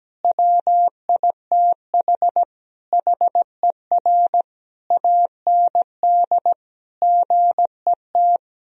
Morse code, heard as a keyed tone, about 17 wpm, 700 Hz. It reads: WITH HER AND GET
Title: Sentences from Top 1000 Words - Review Part 2 - 17wpm